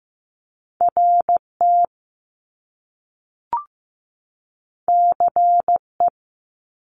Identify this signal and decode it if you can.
Morse code, heard as a keyed tone, about 15 wpm, 700 Hz. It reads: RT CE
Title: Top Two Letter Combinations in Words 15wpm